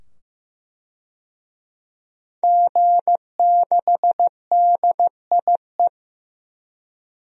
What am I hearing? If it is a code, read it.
Morse code, heard as a keyed tone, about 16 wpm, 700 Hz. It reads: G6DIE
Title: DXCC-100 Call Signs - 5 Characters - 15wpm